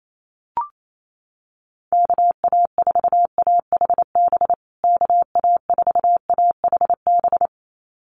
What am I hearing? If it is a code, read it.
Morse code, heard as a keyed tone, about 28 wpm, 700 Hz. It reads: KA4AHB KA4AHB
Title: USA Call Signs Repeated 2x 28wpm